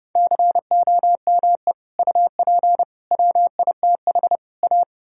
Morse code, heard as a keyed tone, about 29 wpm, 700 Hz. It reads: COME UP WITH A